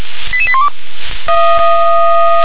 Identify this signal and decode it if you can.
Five-tone selective call (ZVEI), heard as quick runs of five tones, tones of about 70 ms. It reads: 88812